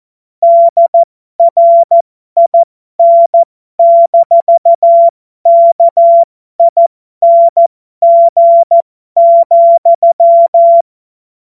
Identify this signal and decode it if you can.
Morse code, heard as a keyed tone, about 14 wpm, spaced as for 12 wpm, 680 Hz. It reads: DRIN-KING,
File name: Morsekurs / Witze